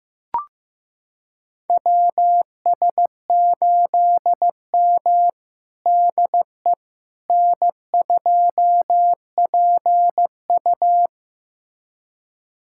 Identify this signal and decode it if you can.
Morse code, heard as a keyed tone, about 15 wpm, 700 Hz. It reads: WS8M DE N2PU
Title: USA Callsign DE Callsign - 4 Characters - 15wpm